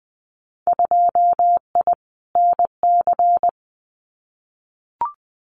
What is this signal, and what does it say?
Morse code, heard as a keyed tone, about 20 wpm, 700 Hz. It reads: 2I NC
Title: Winter Field Day Exchanges 20wpm